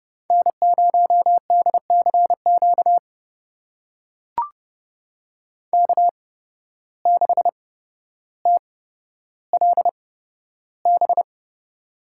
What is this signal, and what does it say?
Morse code, heard as a keyed tone, about 30 wpm, spaced as for 13 wpm, 700 Hz. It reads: N0DCQ K6TLB
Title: USA Call Signs - 5 Characters - 8x Character Spacing - 30wpm